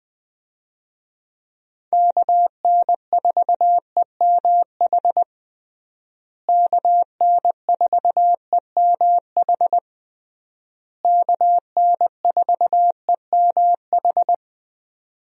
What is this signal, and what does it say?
Morse code, heard as a keyed tone, about 20 wpm, 700 Hz. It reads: KN4EMH KN4EMH KN4EMH
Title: USA Call Signs - 6 Characters - Repeated 3x - 20wpm